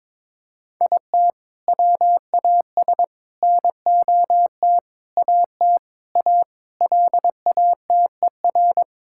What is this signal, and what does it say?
Morse code, heard as a keyed tone, about 22 wpm, 700 Hz. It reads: IT WAS NOT AT A LATER
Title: Sentences from Top 1000 Words - Review Part 5 - 22wpm